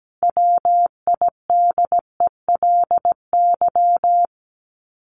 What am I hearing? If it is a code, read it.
Morse code, heard as a keyed tone, about 17 wpm, 700 Hz. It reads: WIDELY
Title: Top 1001-1250 Words 17wpm